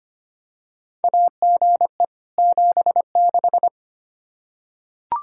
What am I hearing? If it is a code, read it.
Morse code, heard as a keyed tone, about 25 wpm, 700 Hz. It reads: AGE 76
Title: Practice QSO - Exchange 3 - More Info - 25wpm